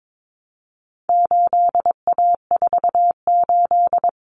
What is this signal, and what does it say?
Morse code, heard as a keyed tone, about 22 wpm, 700 Hz. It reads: 8A48
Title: Common Cut Numbers - 4-Digit 22wpm